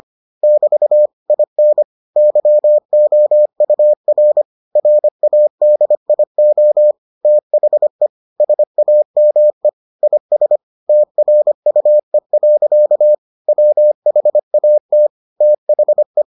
Morse code, heard as a keyed tone, about 25 wpm, 600 Hz. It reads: <BT> IN YOUR RADIO THE SAME IS TRUE. WHAT THE